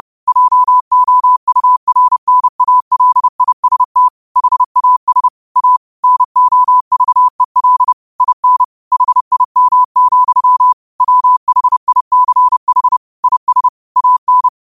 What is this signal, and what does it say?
Morse code, heard as a keyed tone, about 30 wpm, 1000 Hz. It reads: JOURNALIST HAS A NOVEL IN HIM, WHICH IS AN
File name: qotdfiles / 30wpm